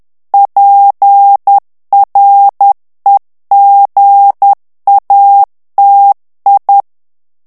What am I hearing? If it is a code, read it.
Morse code, heard as a keyed tone, about 10 wpm, 800 Hz. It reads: PREGATI
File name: ITALIEN texte clair 2 vitesse 10 mots minute